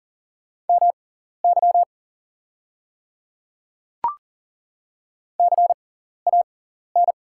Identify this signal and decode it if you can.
Morse code, heard as a keyed tone, about 38 wpm, spaced as for 14 wpm, 700 Hz. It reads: MY CAN
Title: 6x Character Spacing - Top 100 Words - 40wpm